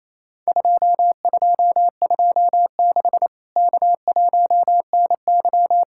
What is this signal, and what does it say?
Morse code, heard as a keyed tone, about 28 wpm, 700 Hz. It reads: 2226 K1NY
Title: Worked All Europe DX Contest - QTC Exchange - 28wpm